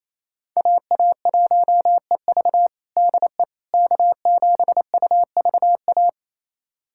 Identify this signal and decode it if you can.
Morse code, heard as a keyed tone, about 28 wpm, 700 Hz. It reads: AA1EV DE K7UVA
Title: USA Callsign DE Callsign - 5 Characters - 28wpm